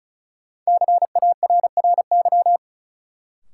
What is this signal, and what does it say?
Morse code, heard as a keyed tone, about 36 wpm, 700 Hz. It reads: CARRY
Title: Sentences from Top 400 Words - New Word Drill - 35wpm